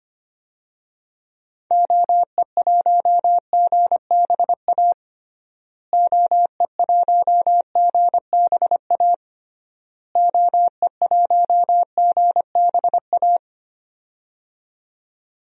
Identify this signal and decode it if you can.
Morse code, heard as a keyed tone, about 25 wpm, 700 Hz. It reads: OE1GBA OE1GBA OE1GBA
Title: DXCC-100 Call Signs - 6 Characters - Repeated 3x - 25wpm